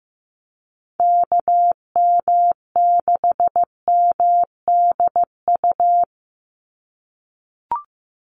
Morse code, heard as a keyed tone, about 15 wpm, 700 Hz. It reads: KM6MDU
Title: USA Call Signs 15wpm